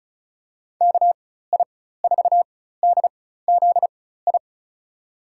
Morse code, heard as a keyed tone, about 36 wpm, spaced as for 16 wpm, 700 Hz. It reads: KI4DZI
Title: USA Call Signs - 6 Characters - 4x Character Spacing - 35wpm